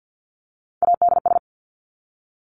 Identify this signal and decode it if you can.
Morse code, heard as a keyed tone, about 48 wpm, 700 Hz. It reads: ADS
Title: Abbreviations 50wpm